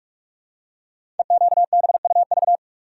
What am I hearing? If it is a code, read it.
Morse code, heard as a keyed tone, about 44 wpm, 690 Hz. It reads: EQBUV
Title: Code Groups - 5 Letters - 45wpm